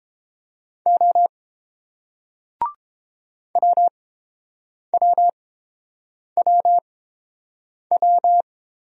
Morse code, heard as a keyed tone, about 29 wpm, spaced as for 7 wpm, 700 Hz. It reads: O WWWW